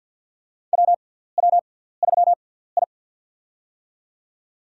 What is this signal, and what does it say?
Morse code, heard as a keyed tone, about 51 wpm, spaced as for 17 wpm, 700 Hz. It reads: WW3I
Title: USA Call Signs - 4 Characters - 6x Character Spacing - 50wpm